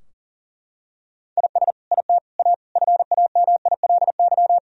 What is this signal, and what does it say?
Morse code, heard as a keyed tone, about 37 wpm, 700 Hz. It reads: IS IT A FAMILY
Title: Sentences from Top 1000 Words - Review Part 4 - 40wpm